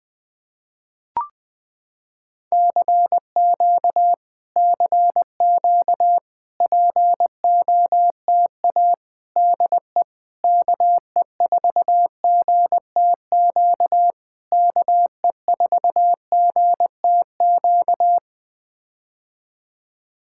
Morse code, heard as a keyed tone, about 20 wpm, 700 Hz. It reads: CQ CQ POTA DE KE4GTQ KE4GTQ
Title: POTA QSO Practice - 20wpm